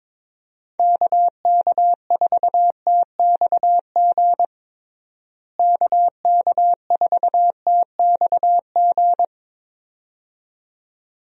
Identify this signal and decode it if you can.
Morse code, heard as a keyed tone, about 22 wpm, 700 Hz. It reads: KK4TXG KK4TXG
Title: USA Call Signs - 6 Characters - Repeated 2x - 22wpm